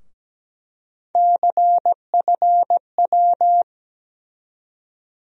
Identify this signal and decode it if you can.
Morse code, heard as a keyed tone, about 17 wpm, 700 Hz. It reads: CFW
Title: USA Call Sign Postfixes - 3 Letters 17wpm